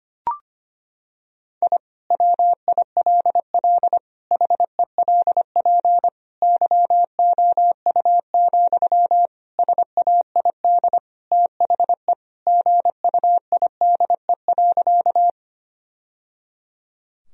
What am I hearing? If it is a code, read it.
Morse code, heard as a keyed tone, about 25 wpm, 700 Hz. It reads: I WILL HELP YOU, SAID THE GUIDE.
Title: Sentences from Top 1000 Words - Review Part 3 - 25wpm